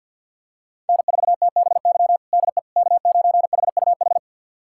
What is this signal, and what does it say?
Morse code, heard as a keyed tone, about 51 wpm, 700 Hz. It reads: N4TBY DE K9HUH
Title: USA Callsign DE Callsign - 5 Characters - 50wpm